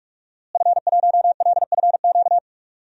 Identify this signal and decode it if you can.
Morse code, heard as a keyed tone, about 41 wpm, 700 Hz. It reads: F1LFQ
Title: DXCC-100 Call Signs - 5 Characters - Repeated 3x - 45wpm